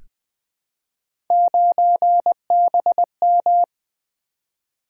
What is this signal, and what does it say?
Morse code, heard as a keyed tone, about 21 wpm, 700 Hz. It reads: 9BM
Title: Three Letter-Number Combinations - 20wpm